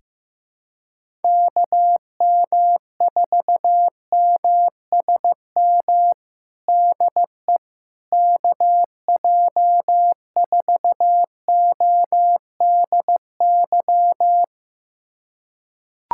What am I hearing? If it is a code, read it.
Morse code, heard as a keyed tone, about 15 wpm, 700 Hz. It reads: KM4MSM DE KJ4ODY
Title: USA Callsign DE Callsign 15wpm